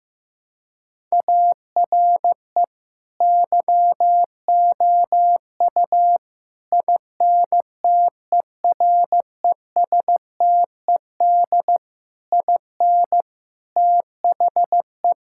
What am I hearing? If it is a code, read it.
Morse code, heard as a keyed tone, about 15 wpm, 700 Hz. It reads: ARE YOU INTERESTED IN THE
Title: Sentences from Top 1000 Words - Review Part 6 - 15wpm